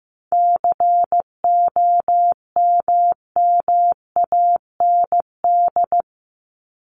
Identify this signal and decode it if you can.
Morse code, heard as a keyed tone, about 15 wpm, 700 Hz. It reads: COMMAND
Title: Top 1501-1750 Words 15wpm